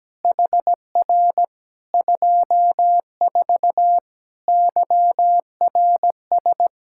Morse code, heard as a keyed tone, about 17 wpm, 700 Hz. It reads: HR 24 YRS
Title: K1USN SST QSO Practice 17wpm